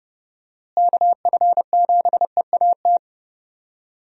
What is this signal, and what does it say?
Morse code, heard as a keyed tone, about 30 wpm, 700 Hz. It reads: KF7EAT
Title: USA Call Signs - 6 Characters - Repeated 3x - 30wpm